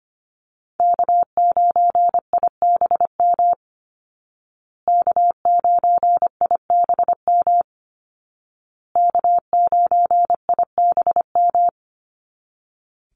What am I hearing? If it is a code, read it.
Morse code, heard as a keyed tone, about 25 wpm, 700 Hz. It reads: K9IBM K9IBM K9IBM
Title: Call Signs Repeated 2x - 25wpm